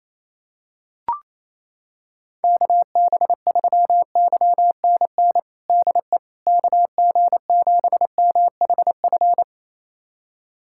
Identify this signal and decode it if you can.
Morse code, heard as a keyed tone, about 28 wpm, 700 Hz. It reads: KB3YNN DE KG7MHF
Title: USA Callsign DE Callsign - 6 Characters - 28wpm